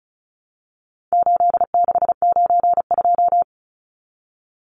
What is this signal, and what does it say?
Morse code, heard as a keyed tone, about 34 wpm, 700 Hz. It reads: 8692
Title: Numbers - 4-Digit 35wpm